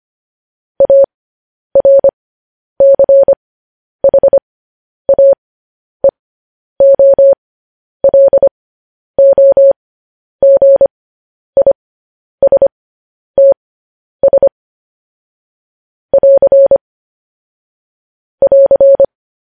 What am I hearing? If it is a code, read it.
Morse code, heard as a keyed tone, about 25 wpm, spaced as for 10 wpm, 550 Hz. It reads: ARCHAEOLOGISTS <AR> <AR>
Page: news-headlines-in-morse-code-at-10-wpm